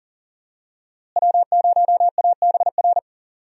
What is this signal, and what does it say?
Morse code, heard as a keyed tone, about 38 wpm, 700 Hz. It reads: W0ABR
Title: USA Call Signs - 5 Characters - Repeated 3x - 40wpm